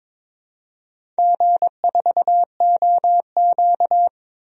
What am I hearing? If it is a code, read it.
Morse code, heard as a keyed tone, about 22 wpm, 700 Hz. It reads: G4OQ
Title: DXCC-100 Call Signs - 4 Characters - 22wpm